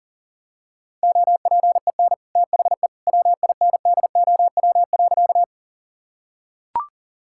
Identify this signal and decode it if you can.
Morse code, heard as a keyed tone, about 37 wpm, 700 Hz. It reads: OPEN THE WINDOW.
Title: Sentences from Top 1600 Words 40wpm